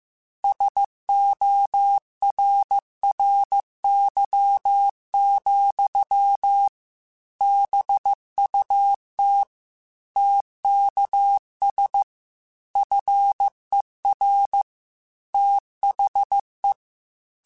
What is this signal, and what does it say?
Morse code, heard as a keyed tone, about 15 wpm, 800 Hz. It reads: SORRY, BUT TKS FER THE
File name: QSOs / QSOs-mp3s / kc7sac@13